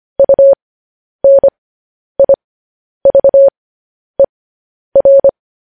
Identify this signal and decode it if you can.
Morse code, heard as a keyed tone, about 25 wpm, spaced as for 10 wpm, 550 Hz. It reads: UNIVER
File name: podcast / News headlines-1x-25x10